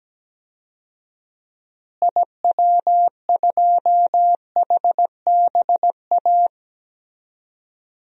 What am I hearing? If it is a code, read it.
Morse code, heard as a keyed tone, about 17 wpm, 700 Hz. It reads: IW2HBA
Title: DXCC-100 Call Signs - 6 Characters - Repeated 3x - 17wpm